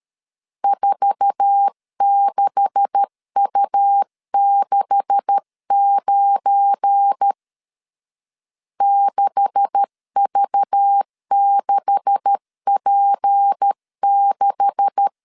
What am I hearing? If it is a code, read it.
Morse code, heard as a keyed tone, about 13 wpm, 800 Hz. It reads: 46U69 6V6P6
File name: LEZIONI / 9 286 12wpm 2305